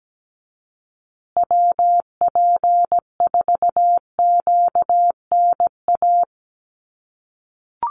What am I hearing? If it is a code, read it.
Morse code, heard as a keyed tone, about 17 wpm, 700 Hz. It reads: WP4QNA
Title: USA Call Signs Repeated 3x 17wpm